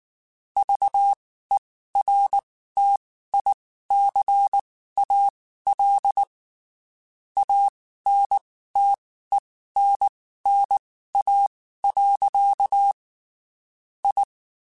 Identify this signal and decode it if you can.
Morse code, heard as a keyed tone, about 19 wpm, spaced as for 14 wpm, 800 Hz. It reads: VERTICAL ANTENNA. I
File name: (spaced) / QSOs Spaced-mp3s / WA7QOX 2 @18 sp